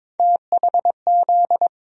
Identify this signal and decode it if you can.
Morse code, heard as a keyed tone, about 21 wpm, 700 Hz. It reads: THZ
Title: USA Call Sign Postfixes - 3 Letters 22wpm